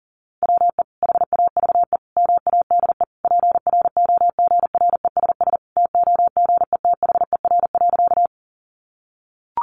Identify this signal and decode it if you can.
Morse code, heard as a keyed tone, about 40 wpm, 700 Hz. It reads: WE HAVE MADE PROGRESS TOGETHER.
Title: Sentences from Top 1400 Words 40wpm